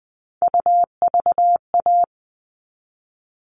Morse code, heard as a keyed tone, about 20 wpm, 700 Hz. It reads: UVA
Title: All Cut Numbers - 3-Digit 20wpm